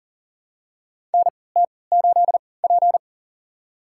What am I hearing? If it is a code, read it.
Morse code, heard as a keyed tone, about 37 wpm, spaced as for 22 wpm, 700 Hz. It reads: NT8P
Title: USA Call Signs - 4 Characters - 3x Character Spacing - 40wpm